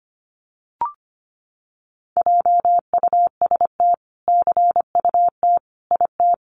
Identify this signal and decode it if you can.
Morse code, heard as a keyed tone, about 25 wpm, 700 Hz. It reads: JUST CUT IT
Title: Sentences from Top 300 Words 25wpm